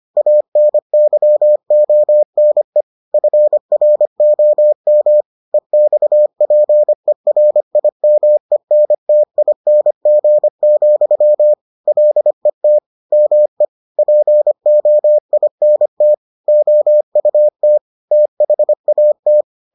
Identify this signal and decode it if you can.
Morse code, heard as a keyed tone, about 25 wpm, 600 Hz. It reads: ANYONE FROM EXPERIMENTING, LET ME POINT OUT THAT